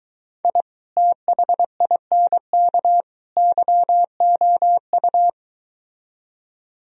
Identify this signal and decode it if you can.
Morse code, heard as a keyed tone, about 23 wpm, 700 Hz. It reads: I THINK YOU